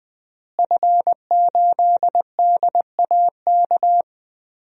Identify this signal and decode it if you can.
Morse code, heard as a keyed tone, about 20 wpm, 700 Hz. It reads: F8DAK
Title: DXCC-100 Call Signs - 5 Characters - Repeated 3x - 20wpm